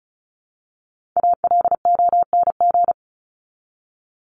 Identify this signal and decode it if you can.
Morse code, heard as a keyed tone, about 34 wpm, 700 Hz. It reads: ALONG
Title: Top 501-750 Words 35wpm